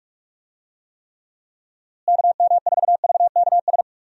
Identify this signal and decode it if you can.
Morse code, heard as a keyed tone, about 45 wpm, 700 Hz. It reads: KM4VKS
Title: USA Call Signs - 6 Characters - Repeated 2x - 45wpm